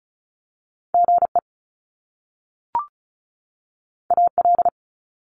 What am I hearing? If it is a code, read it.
Morse code, heard as a keyed tone, about 36 wpm, 700 Hz. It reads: GE AL
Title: Top Two Letter Combinations in Words 35wpm